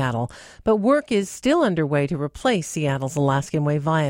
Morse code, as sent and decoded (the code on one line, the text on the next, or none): none